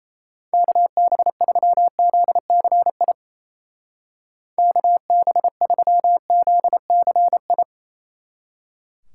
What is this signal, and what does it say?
Morse code, heard as a keyed tone, about 34 wpm, 700 Hz. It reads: KB3ZCI KB3ZCI
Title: Speed Racing - USA 2x3 Call Signs 28wpm